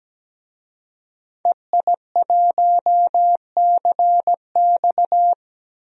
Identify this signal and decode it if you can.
Morse code, heard as a keyed tone, about 17 wpm, 700 Hz. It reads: EI1CX
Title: DXCC-100 Call Signs - 5 Characters - Repeated 2x - 17wpm